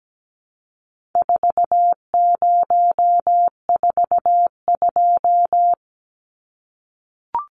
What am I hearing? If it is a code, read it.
Morse code, heard as a keyed tone, about 17 wpm, 700 Hz. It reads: 4042
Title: Numbers - 4-Digit 17wpm